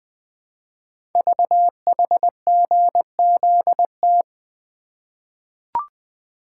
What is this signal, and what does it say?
Morse code, heard as a keyed tone, about 20 wpm, 700 Hz. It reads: VHGZT
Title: Code Groups - 5 Letters - 20wpm